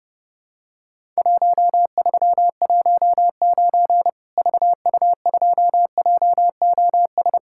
Morse code, heard as a keyed tone, about 30 wpm, 700 Hz. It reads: 1319 VU2JOS
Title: Worked All Europe DX Contest - QTC Exchange - 30wpm